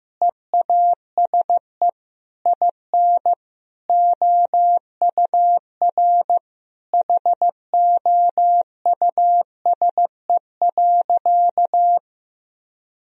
Morse code, heard as a keyed tone, about 15 wpm, 700 Hz. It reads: EASE IN OUR HOUSE.
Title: Sentences from Top 1000 Words - Review Part 1 - 15wpm